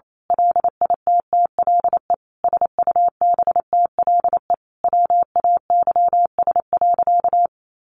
Morse code, heard as a keyed tone, about 28 wpm, 700 Hz. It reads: LITTLE SUBTLE WAYS.
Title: A Sound Of Thunder 28wpm